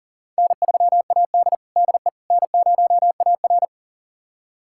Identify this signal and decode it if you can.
Morse code, heard as a keyed tone, about 40 wpm, 700 Hz. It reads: N3AD DE N0AR
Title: USA Callsign DE Callsign - 4 Characters - 40wpm